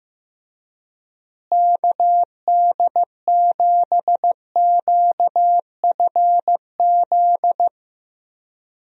Morse code, heard as a keyed tone, about 15 wpm, 700 Hz. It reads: KD7QFZ
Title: USA Call Signs - 6 Characters - Repeated 3x - 15wpm